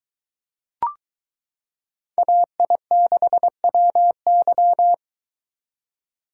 Morse code, heard as a keyed tone, about 22 wpm, 700 Hz. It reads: AI6WY